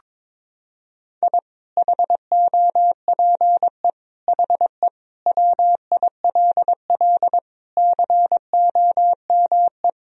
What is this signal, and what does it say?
Morse code, heard as a keyed tone, about 22 wpm, 700 Hz. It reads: I HOPE HE WILL COME
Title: Sentences from Top 1000 Words - Review Part 3 - 22wpm